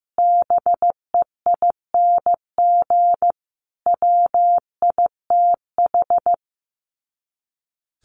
Morse code, heard as a keyed tone, about 15 wpm, 700 Hz. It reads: BEING WITH
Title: Sets of 5 Words Encore 15wpm